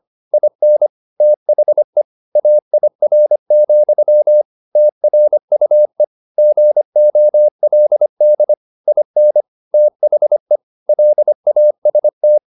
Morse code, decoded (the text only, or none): IN THE AIR, TRUE GOLD IN THE LAST